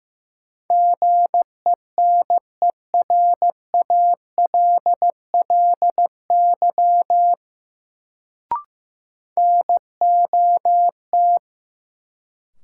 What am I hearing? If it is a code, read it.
Morse code, heard as a keyed tone, about 16 wpm, 700 Hz. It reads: GENERALLY NOT
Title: Sentences from Top 600 Words - New Word Drill - 15wpm